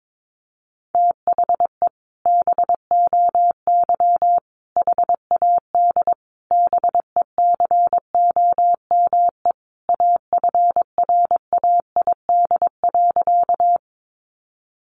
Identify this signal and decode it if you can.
Morse code, heard as a keyed tone, about 22 wpm, 700 Hz. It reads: THE BOY HAD BECOME AFRAID.
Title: Sentences from Top 1500 Words 22wpm